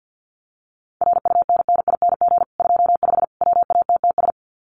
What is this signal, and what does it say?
Morse code, heard as a keyed tone, about 50 wpm, 700 Hz. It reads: RUNNING 25 WATTS
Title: Common QSO Phrases 50wpm